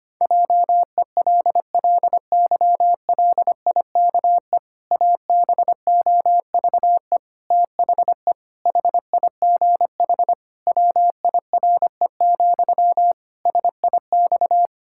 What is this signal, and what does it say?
Morse code, heard as a keyed tone, about 25 wpm, 700 Hz. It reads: JELLYLIKE ABOVE THE HIGH WIRE, SIX
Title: Pangram Challenge v2 - 25wpm